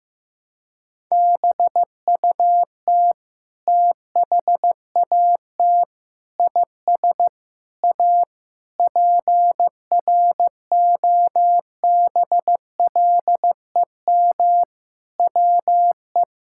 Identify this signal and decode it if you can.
Morse code, heard as a keyed tone, about 15 wpm, 700 Hz. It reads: BUT THAT IS A PROBLEM WE
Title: Sentences from Top 1000 Words - Review Part 6 - 15wpm